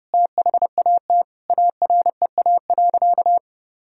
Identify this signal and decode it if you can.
Morse code, heard as a keyed tone, about 30 wpm, 700 Hz. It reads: THAT AREA.
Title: Sentences from Top 1000 Words - Review Part 2 - 30wpm